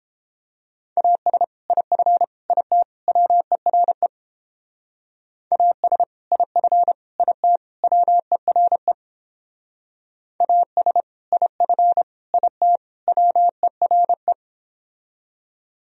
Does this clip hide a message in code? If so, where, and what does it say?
Yes: Morse code from 1.0 to 14.3 s, heard as a keyed tone, about 31 wpm, 700 Hz. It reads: AS IF IT WERE AS IF IT WERE AS IF IT WERE